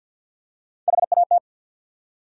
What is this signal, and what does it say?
Morse code, heard as a keyed tone, about 48 wpm, 700 Hz. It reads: HAT
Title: Three Letter Words - 50wpm